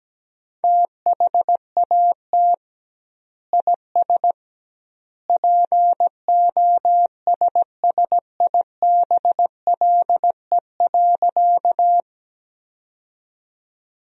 Morse code, heard as a keyed tone, about 17 wpm, 700 Hz. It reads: THAT IS POSSIBLE.
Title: 2x Word-Spacing - Sentences from Top 500 Words - 17wpm